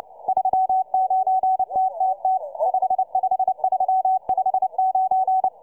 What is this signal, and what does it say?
Morse code, heard as a keyed tone, about 29 wpm, 750 Hz. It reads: 39TTT 55359